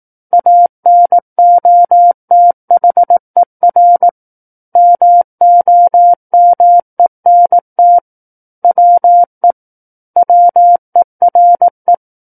Morse code, heard as a keyed tone, about 18 wpm, 700 Hz. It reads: ANOTHER MOMENT WE WERE
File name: 18 WPM